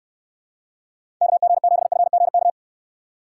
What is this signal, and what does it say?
Morse code, heard as a keyed tone, about 64 wpm, 700 Hz. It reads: KG7FGK